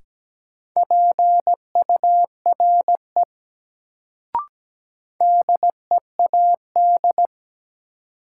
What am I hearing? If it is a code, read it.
Morse code, heard as a keyed tone, about 17 wpm, 700 Hz. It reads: PURE DEAD
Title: Sentences from Top 1500 Words - New Word Drill - 17wpm